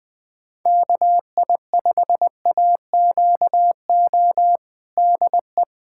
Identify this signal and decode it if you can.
Morse code, heard as a keyed tone, about 20 wpm, 700 Hz. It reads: KI5AQO DE
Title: USA Callsign DE Callsign 20wpm